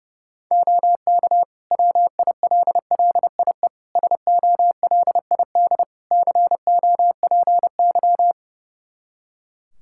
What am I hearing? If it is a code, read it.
Morse code, heard as a keyed tone, about 32 wpm, 700 Hz. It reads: OK WILLIE SOLID COPY
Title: Practice QSO - Exchange 3 - More Info - 30wpm